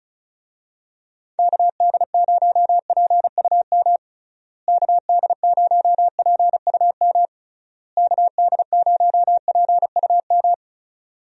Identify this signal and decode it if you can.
Morse code, heard as a keyed tone, about 34 wpm, 700 Hz. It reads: KD0PUM KD0PUM KD0PUM
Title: USA Call Signs - 6 Characters - Repeated 3x - 35wpm